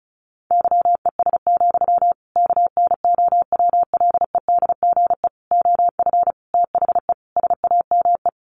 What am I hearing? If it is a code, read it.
Morse code, heard as a keyed tone, about 34 wpm, 700 Hz. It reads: YES, KNOWLEDGE OF THE SAME
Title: Sentences from Top 1500 Words 35wpm